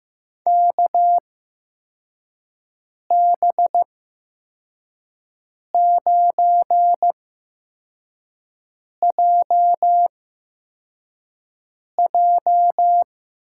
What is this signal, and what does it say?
Morse code, heard as a keyed tone, about 15 wpm, spaced as for 4 wpm, 700 Hz. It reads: KB9JJ
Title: USA Call Signs - 6 Characters - 8x Character Spacing - 15wpm